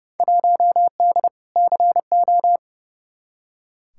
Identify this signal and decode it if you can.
Morse code, heard as a keyed tone, about 30 wpm, 700 Hz. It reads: 1D CO
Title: Field Day Exchanges - 30wpm